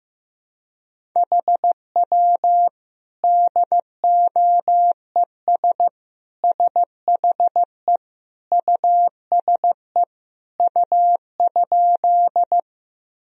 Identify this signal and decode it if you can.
Morse code, heard as a keyed tone, about 16 wpm, 700 Hz. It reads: HW DOES SHE USE U?
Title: Sentences from Top 200 Words and Abbreviations 15wpm